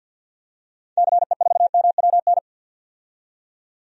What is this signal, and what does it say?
Morse code, heard as a keyed tone, about 47 wpm, 700 Hz. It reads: CE4MWN
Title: DXCC-100 Call Signs - 6 Characters - 50wpm